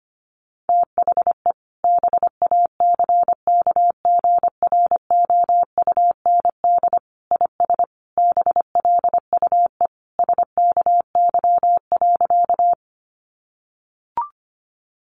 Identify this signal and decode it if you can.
Morse code, heard as a keyed tone, about 25 wpm, 700 Hz. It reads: THE BACKGROUND IS BLUE SKY.
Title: Sentences from Top 1300 Words 25wpm